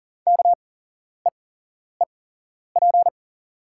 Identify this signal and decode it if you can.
Morse code, heard as a keyed tone, about 40 wpm, spaced as for 11 wpm, 700 Hz. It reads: KEEP
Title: 8x Character Spacing - Top 300 Words - 40wpm